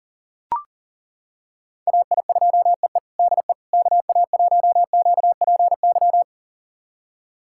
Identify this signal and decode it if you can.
Morse code, heard as a keyed tone, about 40 wpm, 700 Hz. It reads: AI2EE DE KA1QPY
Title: USA Callsign DE Callsign 40wpm